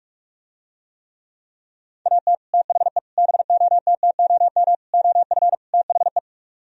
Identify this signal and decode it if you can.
Morse code, heard as a keyed tone, about 45 wpm, 700 Hz. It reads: AT THE BOTTOM OF THE